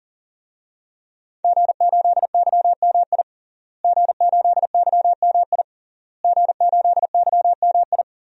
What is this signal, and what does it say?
Morse code, heard as a keyed tone, about 40 wpm, 700 Hz. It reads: G8YMI G8YMI G8YMI
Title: DXCC-100 Call Signs - 5 Characters - Repeated 3x - 40wpm